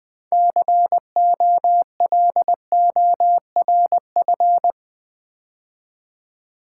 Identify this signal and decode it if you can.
Morse code, heard as a keyed tone, about 20 wpm, 700 Hz. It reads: COLORF
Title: Build A Word - 2 Letters At A Time - 8-Letter Words - 20wpm